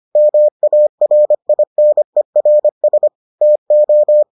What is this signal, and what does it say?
Morse code, heard as a keyed tone, about 25 wpm, 600 Hz. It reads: MARINERS TO